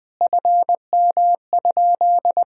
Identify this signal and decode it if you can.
Morse code, heard as a keyed tone, about 20 wpm, 700 Hz. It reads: FM?
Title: Sentences from Top 200 Words and Abbreviations 20wpm